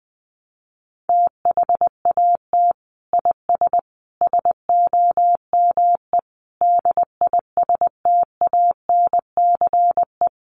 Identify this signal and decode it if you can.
Morse code, heard as a keyed tone, about 20 wpm, 700 Hz. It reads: THAT IS SOME DISTANCE
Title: Sentences from Top 1200 Words 20wpm